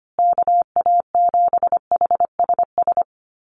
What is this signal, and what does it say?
Morse code, heard as a keyed tone, about 25 wpm, 700 Hz. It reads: KA7HSS
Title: USA Call Signs Repeated 2x 25wpm